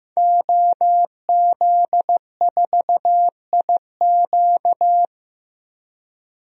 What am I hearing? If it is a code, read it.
Morse code, heard as a keyed tone, about 15 wpm, 700 Hz. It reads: OZ4IQ
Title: DXCC-100 Call Signs - 5 Characters - 15wpm